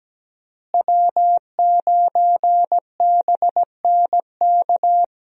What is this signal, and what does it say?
Morse code, heard as a keyed tone, about 17 wpm, 700 Hz. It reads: W9BNK
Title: USA Call Signs - 5 Characters - Repeated 2x - 17wpm